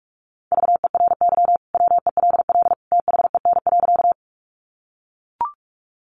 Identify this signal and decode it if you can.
Morse code, heard as a keyed tone, about 44 wpm, 700 Hz. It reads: VERY WELL THEN.
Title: Sentences from Top 200 Words 45wpm